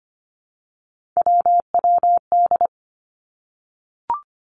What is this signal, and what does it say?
Morse code, heard as a keyed tone, about 25 wpm, 700 Hz. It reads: WWD
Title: Three Challenging Letter-Number Combinations 25wpm